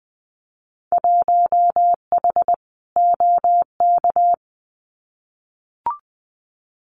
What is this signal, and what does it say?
Morse code, heard as a keyed tone, about 20 wpm, 700 Hz. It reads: 1H OK
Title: Winter Field Day Exchanges 20wpm